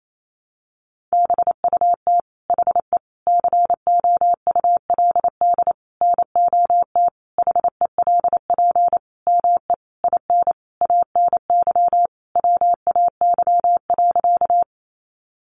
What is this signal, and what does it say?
Morse code, heard as a keyed tone, about 28 wpm, 700 Hz. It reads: BUT HE COULD NOT HELP ME IN ANY WAY.
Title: Sentences from Top 600 Words 28wpm